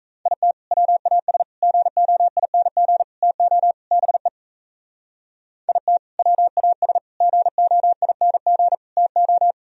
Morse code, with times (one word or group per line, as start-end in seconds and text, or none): 0.3-0.5 IT
0.7-1.4 WAS
1.6-3.0 GOING
3.2-3.7 TO
3.9-4.3 BE
5.7-6.0 IT
6.2-7.0 WAS
7.2-8.8 GOING
9.0-9.5 TO